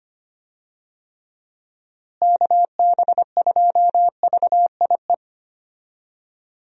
Morse code, heard as a keyed tone, about 25 wpm, 700 Hz. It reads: KB2VIE
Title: USA Call Signs - 6 Characters - Repeated 2x - 25wpm